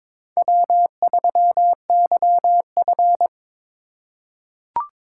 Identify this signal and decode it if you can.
Morse code, heard as a keyed tone, about 22 wpm, 700 Hz. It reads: W3YF
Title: USA Call Signs - 4 Characters - 4x Character Spacing - 22wpm